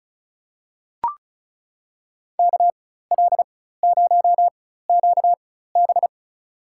Morse code, heard as a keyed tone, about 36 wpm, spaced as for 16 wpm, 700 Hz. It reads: KL0QB
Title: USA Call Signs - 5 Characters - 4x Character Spacing - 35wpm